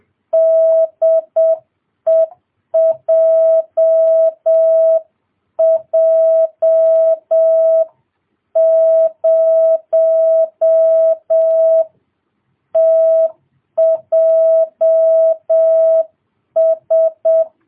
Morse code, heard as a keyed tone, about 7 wpm, 650 Hz. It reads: DE JJ0TJS